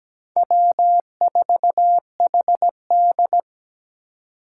Morse code, heard as a keyed tone, about 17 wpm, 700 Hz. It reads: W4HD
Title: Four Challenging Letter-Number Combinations 17wpm